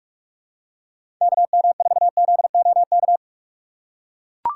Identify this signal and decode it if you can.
Morse code, heard as a keyed tone, about 42 wpm, 700 Hz. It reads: KM4ZOK
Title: USA Call Signs - 6 Characters - 2x Character Spacing - 45wpm